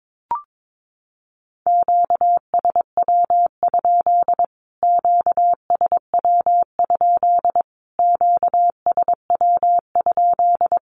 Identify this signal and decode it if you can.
Morse code, heard as a keyed tone, about 22 wpm, 700 Hz. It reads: QSW? QSW? QSW?
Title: Q-Codes Repeated 3x 22wpm